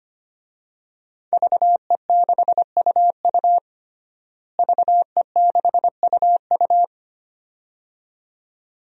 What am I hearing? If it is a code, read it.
Morse code, heard as a keyed tone, about 25 wpm, 700 Hz. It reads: VE6UU VE6UU
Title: DXCC-100 Call Signs - 5 Characters - Repeated 2x - 25wpm